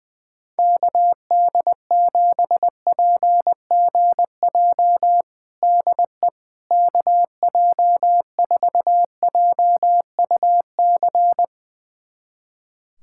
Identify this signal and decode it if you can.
Morse code, heard as a keyed tone, about 20 wpm, 700 Hz. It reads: KD7PGJ DE KJ4JUC
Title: USA Callsign DE Callsign 20wpm